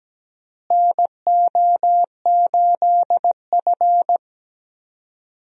Morse code, heard as a keyed tone, about 17 wpm, 700 Hz. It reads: NO8F
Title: USA Call Signs - 4 Characters - Repeated 3x - 17wpm